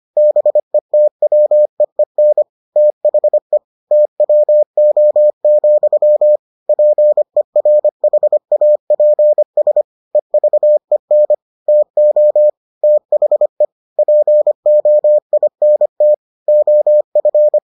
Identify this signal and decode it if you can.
Morse code, heard as a keyed tone, about 25 wpm, 600 Hz. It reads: BETWEEN THE TWO, PERHAPS EVEN TO THE POINT OF